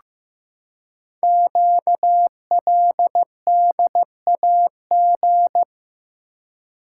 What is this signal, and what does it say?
Morse code, heard as a keyed tone, about 15 wpm, 700 Hz. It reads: QLDAG
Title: Code Groups - 5 Letters - 15wpm